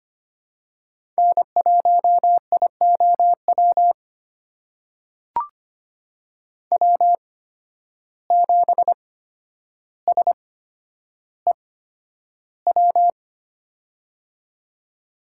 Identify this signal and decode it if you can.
Morse code, heard as a keyed tone, about 25 wpm, spaced as for 11 wpm, 700 Hz. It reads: N1IOW W7SEW
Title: USA Call Signs - 5 Characters - 8x Character Spacing - 25wpm